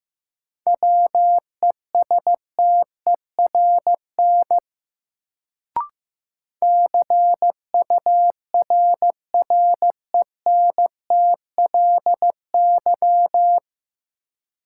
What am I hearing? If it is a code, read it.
Morse code, heard as a keyed tone, about 15 wpm, 700 Hz. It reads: WESTERN CURRENTLY
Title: Sentences from Top 1000 Words - Review Part 7 - New Word Drill - 15wpm